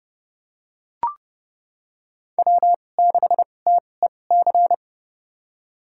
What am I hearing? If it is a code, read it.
Morse code, heard as a keyed tone, about 30 wpm, spaced as for 22 wpm, 700 Hz. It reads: W6TEC
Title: USA Call Signs - 5 Characters - 2x Character Spacing - 30wpm